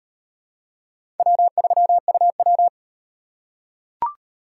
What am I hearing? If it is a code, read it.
Morse code, heard as a keyed tone, about 35 wpm, 700 Hz. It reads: W3UW